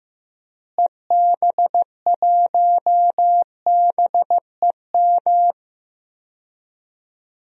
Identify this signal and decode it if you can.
Morse code, heard as a keyed tone, about 15 wpm, 700 Hz. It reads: EB1BEM
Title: DXCC-100 Call Signs - 6 Characters - Repeated 3x - 15wpm